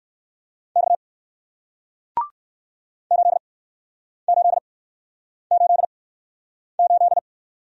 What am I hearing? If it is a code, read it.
Morse code, heard as a keyed tone, about 60 wpm, spaced as for 9 wpm, 700 Hz. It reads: X 8888